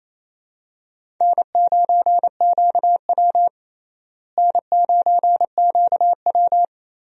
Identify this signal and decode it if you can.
Morse code, heard as a keyed tone, about 27 wpm, 700 Hz. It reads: N9QW N9QW
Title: USA Call Signs - 4 Characters - Repeated 2x - 28wpm